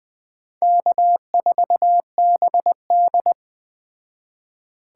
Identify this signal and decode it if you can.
Morse code, heard as a keyed tone, about 21 wpm, 700 Hz. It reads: K4BD
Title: Speed Racing - USA 1x2 Call Signs 17wpm